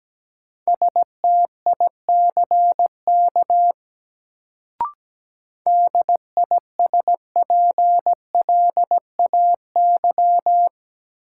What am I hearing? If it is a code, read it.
Morse code, heard as a keyed tone, about 17 wpm, 700 Hz. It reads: STICK DISPLAY
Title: Sentences from Top 1200 Words - New Word Drill Alt - 17wpm